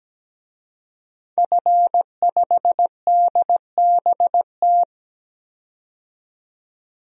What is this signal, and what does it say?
Morse code, heard as a keyed tone, about 17 wpm, 700 Hz. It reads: F5DBT
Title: DXCC-100 Call Signs - 5 Characters - 17wpm